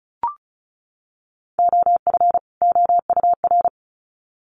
Morse code, heard as a keyed tone, about 34 wpm, 700 Hz. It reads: OF OUR
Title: Sets of 2 Words 35wpm